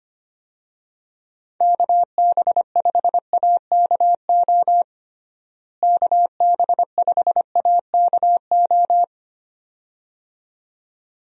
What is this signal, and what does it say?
Morse code, heard as a keyed tone, about 25 wpm, 700 Hz. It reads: KB5AKO KB5AKO
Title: USA Call Signs - 6 Characters - Repeated 2x - 25wpm